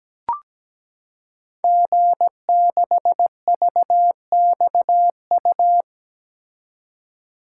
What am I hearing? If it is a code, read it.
Morse code, heard as a keyed tone, about 17 wpm, 700 Hz. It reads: G6VXU
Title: Code Groups - 5 Letter:Numbers - 17wpm